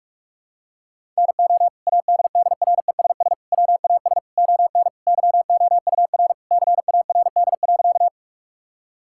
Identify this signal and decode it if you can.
Morse code, heard as a keyed tone, about 43 wpm, 700 Hz. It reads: NO ADDRESS WAS ON YOUR CARD.
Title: Sentences from Top 1000 Words - Review Part 6 - 45wpm